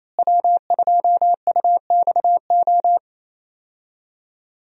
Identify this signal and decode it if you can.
Morse code, heard as a keyed tone, about 28 wpm, 700 Hz. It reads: W2UXO
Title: USA Call Signs - 5 Characters - Repeated 2x - 28wpm